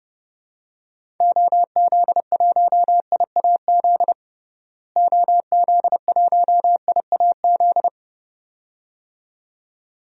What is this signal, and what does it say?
Morse code, heard as a keyed tone, about 30 wpm, 700 Hz. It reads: OZ1IAZ OZ1IAZ
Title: DXCC-100 Call Signs - 6 Characters - Repeated 2x - 30wpm